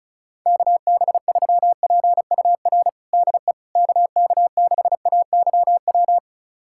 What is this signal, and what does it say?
Morse code, heard as a keyed tone, about 36 wpm, 700 Hz. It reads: KB3PUR DE KK6AYW
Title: USA Callsign DE Callsign - 6 Characters - 35wpm